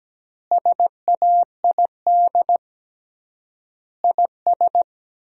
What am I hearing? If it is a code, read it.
Morse code, heard as a keyed tone, about 17 wpm, 700 Hz. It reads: SAID IS
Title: 3x Word-Spacing - Sentences from Top 200 Words - 17wpm